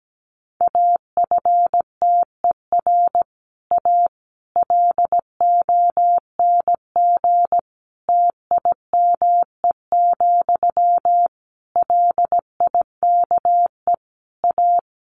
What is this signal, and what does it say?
Morse code, heard as a keyed tone, about 17 wpm, 700 Hz. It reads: AFTER A LONG TIME, LIKE A
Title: A Sound Of Thunder 17wpm